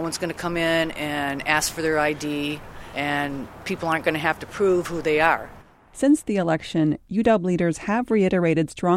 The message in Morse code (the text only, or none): none